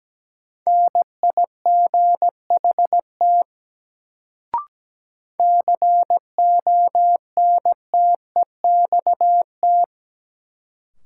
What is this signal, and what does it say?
Morse code, heard as a keyed tone, about 17 wpm, 700 Hz. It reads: NIGHT CONTEXT
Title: Sets of 4 Words - New Word Drill - 17wpm